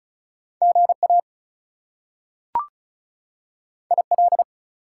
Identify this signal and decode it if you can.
Morse code, heard as a keyed tone, about 36 wpm, 700 Hz. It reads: GA IL
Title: US States and Canadian Provinces 35wpm